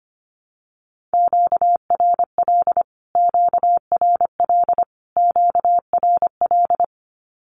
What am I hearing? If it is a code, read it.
Morse code, heard as a keyed tone, about 25 wpm, 700 Hz. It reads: QRL QRL QRL
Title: Q-Codes Repeated 3x 25wpm